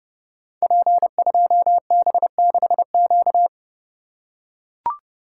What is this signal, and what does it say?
Morse code, heard as a keyed tone, about 30 wpm, 700 Hz. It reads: P2B6Q
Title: Code Groups - 5 Letter:Numbers - 30wpm